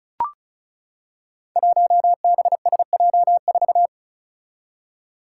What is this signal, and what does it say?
Morse code, heard as a keyed tone, about 36 wpm, 700 Hz. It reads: 1BSJ4
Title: Five Challenging Letter-Number Combinations 35wpm